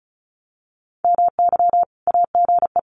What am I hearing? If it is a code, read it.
Morse code, heard as a keyed tone, about 34 wpm, 700 Hz. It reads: MY AGE
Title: Common QSO Phrases Encore 35wpm